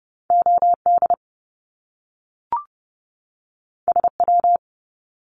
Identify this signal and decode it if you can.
Morse code, heard as a keyed tone, about 30 wpm, 700 Hz. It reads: OD SW
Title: Top Two Letter Combinations in Words 30wpm